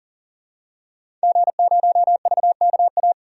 Morse code, heard as a keyed tone, about 40 wpm, 700 Hz. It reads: G0VKA
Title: DXCC-100 Call Signs - 5 Characters - 40wpm